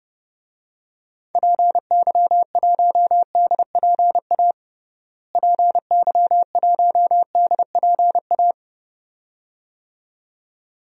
Morse code, heard as a keyed tone, about 29 wpm, 700 Hz. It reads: PY1DPA PY1DPA
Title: DXCC-100 Call Signs - 6 Characters - Repeated 2x - 30wpm